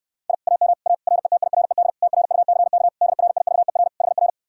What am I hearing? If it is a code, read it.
Morse code, heard as a keyed tone, about 67 wpm, 700 Hz. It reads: I AM A RETIRED TRUCK DRIVER HR